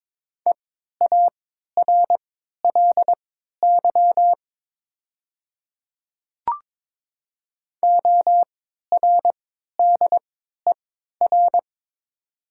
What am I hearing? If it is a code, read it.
Morse code, heard as a keyed tone, about 22 wpm, spaced as for 12 wpm, 700 Hz. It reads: EARLY ORDER
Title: 3x Character Spacing - Top 500 Words - 22wpm